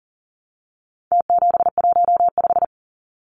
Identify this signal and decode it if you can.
Morse code, heard as a keyed tone, about 40 wpm, 700 Hz. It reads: T715
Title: Numbers - Contest Style - 4-Digit 40wpm